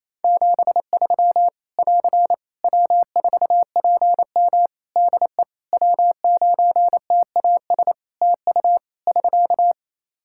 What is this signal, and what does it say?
Morse code, heard as a keyed tone, about 28 wpm, 700 Hz. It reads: 73 <AR> W4PM DE W9TAS TU <SK>
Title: K1USN SST QSO Practice 28wpm